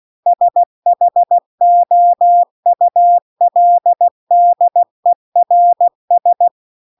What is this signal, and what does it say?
Morse code, heard as a keyed tone, about 16 wpm, 700 Hz. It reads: SHOULDERS
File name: CW-The-Big-Twoheaded-River-2-16WPM